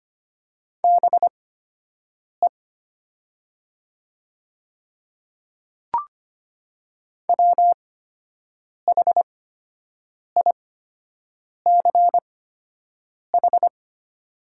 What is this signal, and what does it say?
Morse code, heard as a keyed tone, about 25 wpm, spaced as for 7 wpm, 700 Hz. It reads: BE WHICH
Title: 8x Character Spacing - Top 100 Words - 25wpm